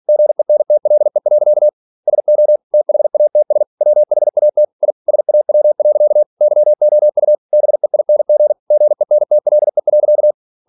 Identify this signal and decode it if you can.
Morse code, heard as a keyed tone, about 46 wpm, 600 Hz. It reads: GENTLE. SO THATS WHAT I SAW. YOU BEING GENTLE.
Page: Never 47wpm